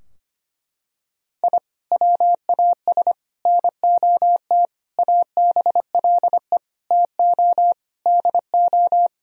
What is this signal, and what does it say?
Morse code, heard as a keyed tone, about 25 wpm, 700 Hz. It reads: I WAS NOT ABLE TO DO
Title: Sentences from Top 1000 Words - Review Part 2 - 25wpm